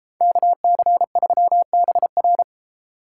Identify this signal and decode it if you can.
Morse code, heard as a keyed tone, about 33 wpm, 700 Hz. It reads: KC3BR